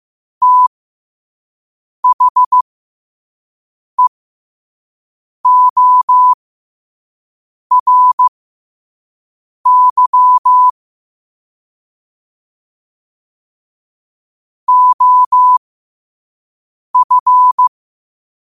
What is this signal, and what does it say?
Morse code, heard as a keyed tone, about 15 wpm, spaced as for 5 wpm, 1000 Hz. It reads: THEORY OF